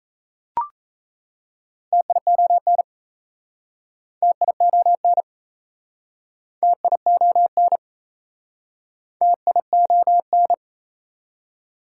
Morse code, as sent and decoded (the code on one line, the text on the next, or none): - .. --- -. / - .. --- -. / - .. --- -. / - .. --- -.
TION TION TION TION